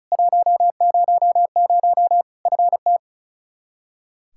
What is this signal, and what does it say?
Morse code, heard as a keyed tone, about 36 wpm, 700 Hz. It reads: 100 FT
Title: Practice QSO - Exchange 2 - Info - 35wpm